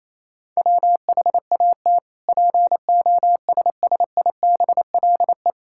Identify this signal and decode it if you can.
Morse code, heard as a keyed tone, about 28 wpm, 700 Hz. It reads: WHAT POSSIBLE
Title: Sentences from Top 1000 Words - Review Part 5 - 28wpm